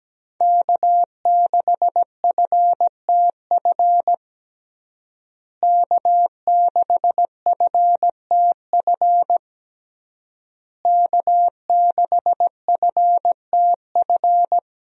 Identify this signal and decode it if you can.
Morse code, heard as a keyed tone, about 17 wpm, 700 Hz. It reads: K6FTF K6FTF K6FTF
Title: USA Call Signs - 5 Characters - Repeated 3x - 17wpm